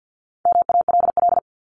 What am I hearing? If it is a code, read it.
Morse code, heard as a keyed tone, about 50 wpm, 700 Hz. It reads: MALL
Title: Top 1501-1750 Words 50wpm